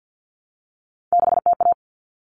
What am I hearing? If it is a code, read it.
Morse code, heard as a keyed tone, about 50 wpm, 700 Hz. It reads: 6TA
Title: All Cut Numbers - 3-Digit 50wpm